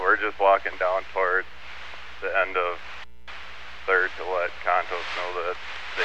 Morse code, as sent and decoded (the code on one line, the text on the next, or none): none